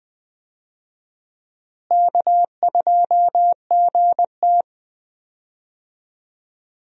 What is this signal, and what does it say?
Morse code, heard as a keyed tone, about 21 wpm, 700 Hz. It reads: K2GT
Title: USA Call Signs - 4 Characters - Repeated 3x - 20wpm